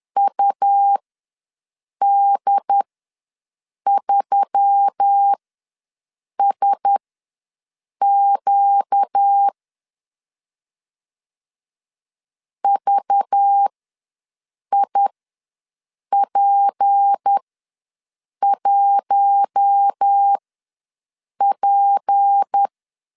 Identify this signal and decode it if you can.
Morse code, heard as a keyed tone, about 10 wpm, spaced as for 6 wpm, 800 Hz. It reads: UD3SQ VIP1P
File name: LEZIONI / 6 PKX0 / 6 P ws 10wpm 3751